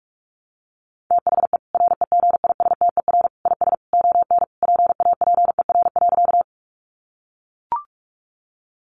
Morse code, heard as a keyed tone, about 44 wpm, 700 Hz. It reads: THE REGISTER IS ON PAPER.
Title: Sentences from Top 1200 Words 45wpm